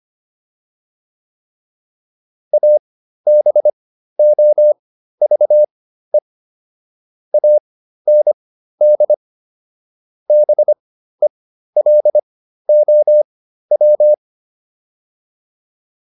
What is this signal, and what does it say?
Morse code, heard as a keyed tone, about 25 wpm, spaced as for 13 wpm, 600 Hz. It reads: ABOVE AND BELOW
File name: PR404 13